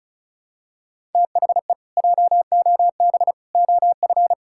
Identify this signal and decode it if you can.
Morse code, heard as a keyed tone, about 34 wpm, 700 Hz. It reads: THE JOB OF
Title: Pangram Challenge v2 - 35wpm